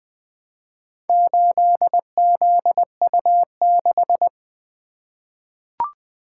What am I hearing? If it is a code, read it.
Morse code, heard as a keyed tone, about 20 wpm, 700 Hz. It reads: 8ZU6
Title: Four Letter-Number Combinations - 20wpm